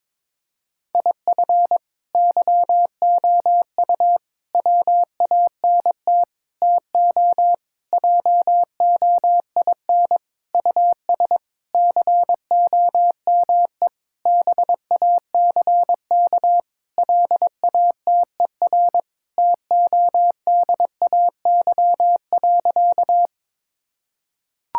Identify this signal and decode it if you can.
Morse code, heard as a keyed tone, about 22 wpm, 700 Hz. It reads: IF YOU WANT TO JOIN US COME BACK LATER TODAY.
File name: practice / 22wpm / Sentences with 4-Word Phrases - 22wpm